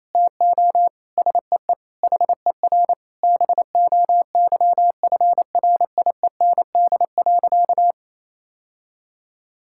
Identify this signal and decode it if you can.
Morse code, heard as a keyed tone, about 28 wpm, 700 Hz. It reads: TO SEE HER BOYFRIEND.
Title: Sentences from Top 1000 Words - Review Part 5 - 28wpm